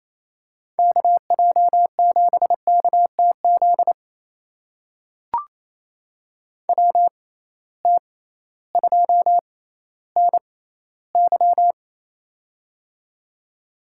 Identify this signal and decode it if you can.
Morse code, heard as a keyed tone, about 27 wpm, spaced as for 15 wpm, 700 Hz. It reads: KJ7KTZ WT2NY
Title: USA Call Signs - 6x Character Spacing - 28wpm